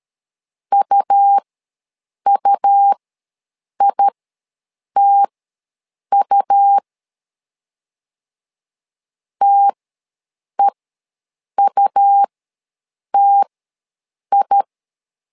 Morse code, heard as a keyed tone, about 13 wpm, spaced as for 7 wpm, 800 Hz. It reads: UUITU TEUTI
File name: LEZIONI / 2 UHDN / 2 U ws 12wpm 1655